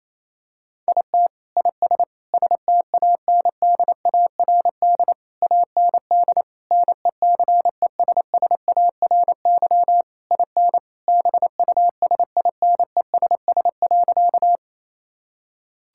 Morse code, heard as a keyed tone, about 28 wpm, 700 Hz. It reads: IT IS STANDARD AND NECESSARY IN BUSINESS.
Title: Sentences from Top 1000 Words - Review Part 6 - 28wpm